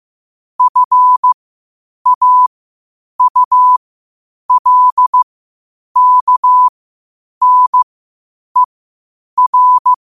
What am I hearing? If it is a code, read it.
Morse code, heard as a keyed tone, about 15 wpm, spaced as for 8 wpm, 1000 Hz. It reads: FAULKNER
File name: qotdfiles / 8wpm